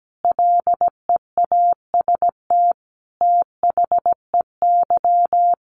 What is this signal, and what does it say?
Morse code, heard as a keyed tone, about 17 wpm, 700 Hz. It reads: LEAST THEY
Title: Sentences from Top 700 Words 17wpm